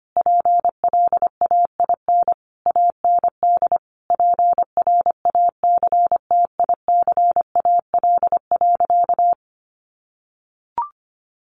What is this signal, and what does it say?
Morse code, heard as a keyed tone, about 25 wpm, 700 Hz. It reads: PLAIN AND PRACTICAL.
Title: Sentences from Top 1400 Words 25wpm